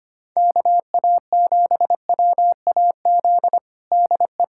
Morse code, heard as a keyed tone, about 25 wpm, 700 Hz. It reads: KA7WAZ DE
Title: USA Callsign DE Callsign - 6 Characters - 25wpm